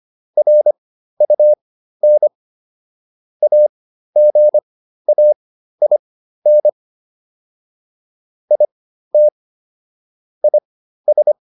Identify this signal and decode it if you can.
Morse code, heard as a keyed tone, about 25 wpm, spaced as for 13 wpm, 600 Hz. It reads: RUN AGAIN IT IS